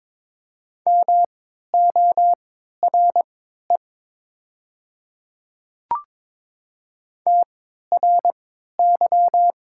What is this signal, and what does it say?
Morse code, heard as a keyed tone, about 22 wpm, spaced as for 13 wpm, 700 Hz. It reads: MORE TRY
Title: 3x Character Spacing - Top 200 Words - 22wpm